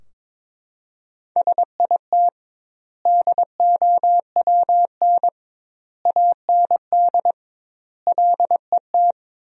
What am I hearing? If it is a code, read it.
Morse code, heard as a keyed tone, about 21 wpm, 700 Hz. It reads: SIT DOWN AND LET
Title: 2x Word-Spacing - Sentences from Top 500 Words - 22wpm